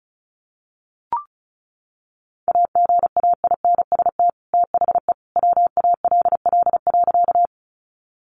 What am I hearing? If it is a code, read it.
Morse code, heard as a keyed tone, about 34 wpm, 700 Hz. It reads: AGAINST THE WALL.
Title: Sentences from Top 1100 Words 35wpm